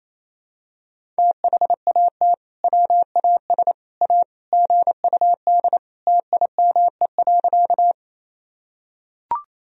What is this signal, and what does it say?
Morse code, heard as a keyed tone, about 27 wpm, 700 Hz. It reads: THAT WAS A GUD TIME.
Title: Sentences from Top 200 Words and Abbreviations 28wpm